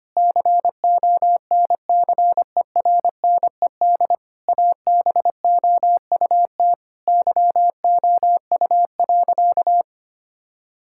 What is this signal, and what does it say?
Morse code, heard as a keyed tone, about 25 wpm, 700 Hz. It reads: CONCERNED ABOUT YOU.
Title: Sentences from Top 1000 Words - Review Part 6 - 25wpm